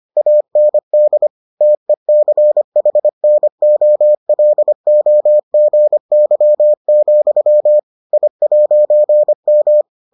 Morse code, heard as a keyed tone, about 25 wpm, 600 Hz. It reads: AND TECHNOLOGY, I'M